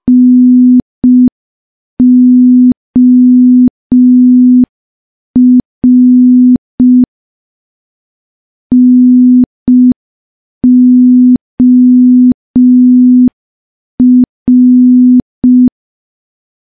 Morse code, heard as a keyed tone, about 5 wpm, 250 Hz. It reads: NOR NOR
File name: morse / morse-training-05WPM